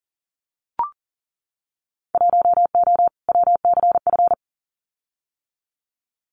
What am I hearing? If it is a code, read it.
Morse code, heard as a keyed tone, about 40 wpm, 700 Hz. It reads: 1O WCF
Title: Winter Field Day Exchanges 40wpm